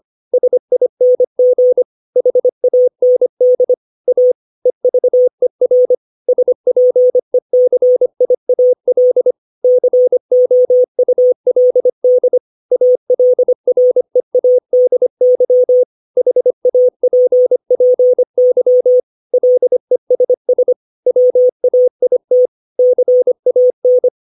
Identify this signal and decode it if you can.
Morse code, heard as a keyed tone, about 25 wpm, 500 Hz. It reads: SING HAND A EVER SPECIAL COULD ALREADY HAPPY LESS WAIT CAN
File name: 632 words / 632 Words 25WPM